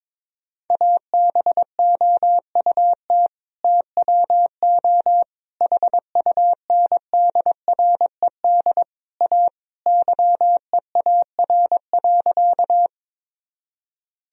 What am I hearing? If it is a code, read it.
Morse code, heard as a keyed tone, about 22 wpm, 700 Hz. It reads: ABOUT TWO HUNDRED A YEAR.
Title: Sentences from Top 1000 Words - Review Part 2 - 22wpm